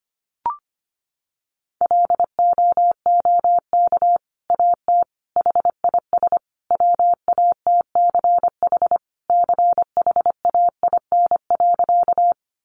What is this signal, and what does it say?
Morse code, heard as a keyed tone, about 25 wpm, 700 Hz. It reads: LOOK AT HIS WATCH CHAIN.
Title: Sentences from Top 1500 Words 25wpm